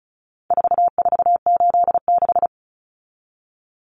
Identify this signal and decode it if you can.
Morse code, heard as a keyed tone, about 34 wpm, 700 Hz. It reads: 4486
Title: Numbers - 4-Digit 35wpm